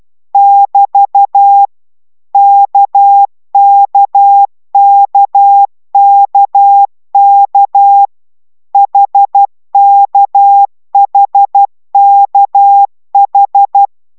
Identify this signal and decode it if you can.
Morse code, heard as a keyed tone, about 12 wpm, 800 Hz. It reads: <BT> KKKKK HKHKH